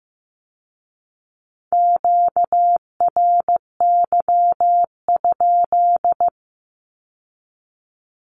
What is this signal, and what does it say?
Morse code, heard as a keyed tone, about 15 wpm, 700 Hz. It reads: QRY?
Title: Q-Codes with Spoken Meaning - 15wpm